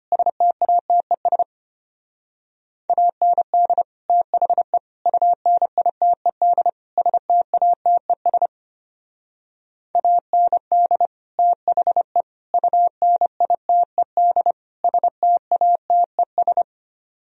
Morse code, heard as a keyed tone, about 33 wpm, 700 Hz. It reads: STATES AND THE UNITED STATES AND THE UNITED STATES